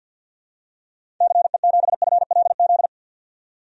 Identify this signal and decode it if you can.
Morse code, heard as a keyed tone, about 50 wpm, 700 Hz. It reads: CE7FLZ
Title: DXCC-100 Call Signs - 6 Characters - 50wpm